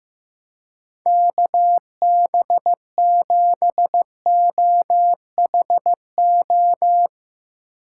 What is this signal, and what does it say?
Morse code, heard as a keyed tone, about 15 wpm, 700 Hz. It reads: KB7OHO
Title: Speed Racing - USA 2x3 Call Signs 15wpm